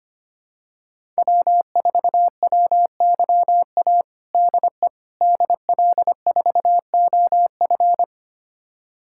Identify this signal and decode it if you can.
Morse code, heard as a keyed tone, about 25 wpm, 700 Hz. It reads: W4WYA DE DL4OF
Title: USA Callsign DE Callsign - 5 Characters - 25wpm